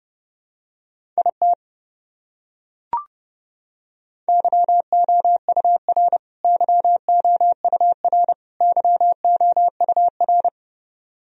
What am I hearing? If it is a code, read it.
Morse code, heard as a keyed tone, about 30 wpm, 700 Hz. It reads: IT YOUR YOUR YOUR
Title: Top 100 Words Repeated 30wpm